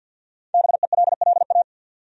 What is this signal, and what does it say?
Morse code, heard as a keyed tone, about 53 wpm, 700 Hz. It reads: BELLA
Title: USA Common Names - 50wpm